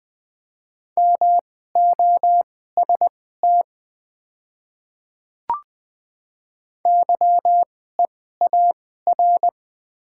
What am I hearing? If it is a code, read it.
Morse code, heard as a keyed tone, about 20 wpm, spaced as for 15 wpm, 700 Hz. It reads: MOST YEAR
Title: 2x Character Spacing - Top 200 Words - 20wpm